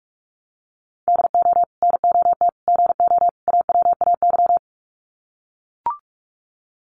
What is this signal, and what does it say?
Morse code, heard as a keyed tone, about 44 wpm, 700 Hz. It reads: DO NOT GO AWAY
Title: Sets of 4 Words 45wpm